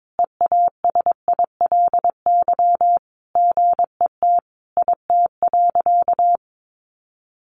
Morse code, heard as a keyed tone, about 22 wpm, 700 Hz. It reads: EASILY GET IT.
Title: Sentences from Top 1500 Words 22wpm